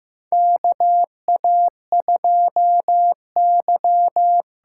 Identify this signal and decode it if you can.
Morse code, heard as a keyed tone, about 15 wpm, 700 Hz. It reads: KA2Y
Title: USA Call Signs - 6 Characters - 8x Character Spacing - 15wpm